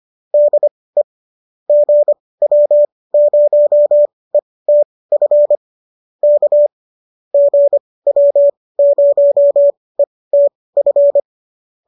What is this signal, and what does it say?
Morse code, heard as a keyed tone, about 25 wpm, spaced as for 18 wpm, 590 Hz. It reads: DE GW0ETF K GW0ETF
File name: qso218 18